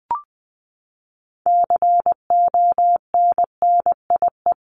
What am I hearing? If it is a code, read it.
Morse code, heard as a keyed tone, about 20 wpm, 700 Hz. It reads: CONNIE
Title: CWT Exchange 20wpm